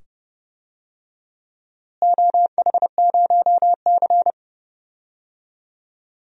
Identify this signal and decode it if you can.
Morse code, heard as a keyed tone, about 30 wpm, 700 Hz. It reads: OH0C
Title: DXCC-100 Call Signs - 4 Characters - Repeated 2x - 30wpm